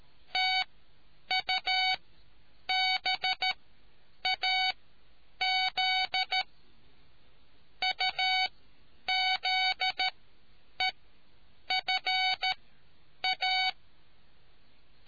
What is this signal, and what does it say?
Morse code, heard as a keyed tone, about 13 wpm, spaced as for 8 wpm, 750 Hz. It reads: TUBAZ UZEFA